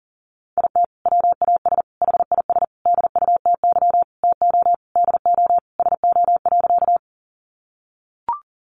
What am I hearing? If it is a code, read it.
Morse code, heard as a keyed tone, about 40 wpm, 700 Hz. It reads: IT WAS HIS DUTY TO DO SO.
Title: Sentences from Top 1500 Words 40wpm